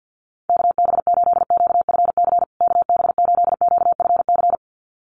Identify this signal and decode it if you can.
Morse code, heard as a keyed tone, about 50 wpm, 700 Hz. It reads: KB8QFC KB8QFC
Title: USA Call Signs Repeated 2x 50wpm